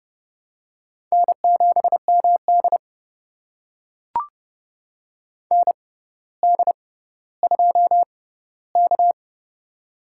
Morse code, heard as a keyed tone, about 30 wpm, spaced as for 15 wpm, 700 Hz. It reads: N7MD ND2K
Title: USA Call Signs - 4 Characters - 6x Character Spacing - 30wpm